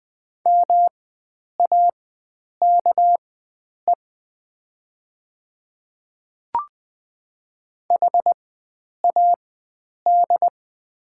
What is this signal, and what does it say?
Morse code, heard as a keyed tone, about 20 wpm, spaced as for 9 wpm, 700 Hz. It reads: MAKE HAD
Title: 4x Character Spacing - Top 500 Words - 20wpm